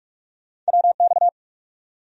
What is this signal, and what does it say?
Morse code, heard as a keyed tone, about 60 wpm, spaced as for 52 wpm, 700 Hz. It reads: WX
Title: Speed-Racing Abbreviations - 45wpm